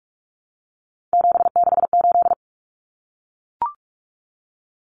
Morse code, heard as a keyed tone, about 44 wpm, 700 Hz. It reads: 768
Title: Common Cut Numbers - 3-Digit 45wpm